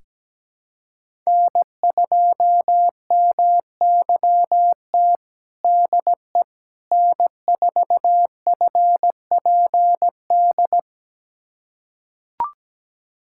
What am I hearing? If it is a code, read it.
Morse code, heard as a keyed tone, about 17 wpm, 700 Hz. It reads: N2MYT DE N4FPD
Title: USA Callsign DE Callsign - 5 Characters - 17wpm